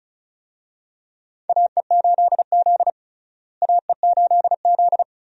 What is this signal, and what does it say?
Morse code, heard as a keyed tone, about 36 wpm, 700 Hz. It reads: AE8Z AE8Z
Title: USA Call Signs - 4 Characters - Repeated 2x - 35wpm